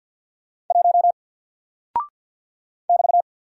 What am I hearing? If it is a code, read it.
Morse code, heard as a keyed tone, about 48 wpm, 700 Hz. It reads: 1 <BT>
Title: Single Character (Letter:Number:Punctuation:Prosign) 50wpm